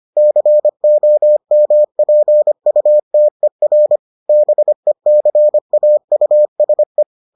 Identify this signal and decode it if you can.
Morse code, heard as a keyed tone, about 25 wpm, 600 Hz. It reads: COMPUTER BECAUSE